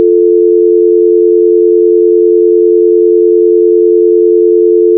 Dial tone (North America), 350 + 440 Hz.
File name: fear-the-spotlight-fx-linux-macos-ps4-ps5-switch-windows-xbox-one-gamerip-2024 / chmrtdsnvs / 593. Ramicio Dial-Tone